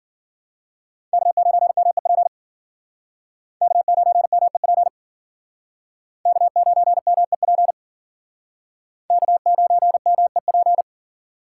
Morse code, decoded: K9MEP K9MEP K9MEP K9MEP

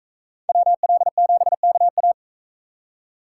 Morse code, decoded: WL7KA

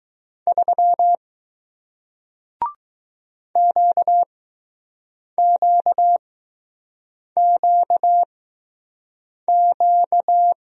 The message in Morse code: 3 QQQQ

